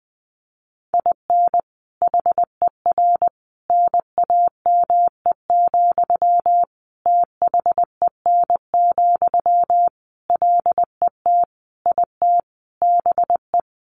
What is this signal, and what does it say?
Morse code, heard as a keyed tone, about 20 wpm, 700 Hz. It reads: IN HER NAME, THEN, LET IT BE
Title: Sentences from Top 500 Words 20wpm